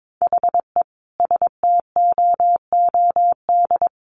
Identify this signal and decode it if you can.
Morse code, heard as a keyed tone, about 22 wpm, 700 Hz. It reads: HE STOOD
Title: A Sound Of Thunder 22wpm